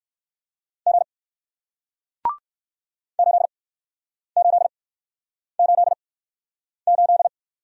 Morse code, heard as a keyed tone, about 60 wpm, spaced as for 9 wpm, 700 Hz. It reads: G 8888